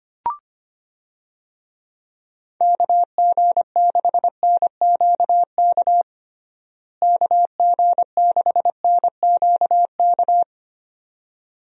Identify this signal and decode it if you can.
Morse code, heard as a keyed tone, about 25 wpm, 700 Hz. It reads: KG6NQK KG6NQK
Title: USA Call Signs - 6 Characters - Repeated 2x - 25wpm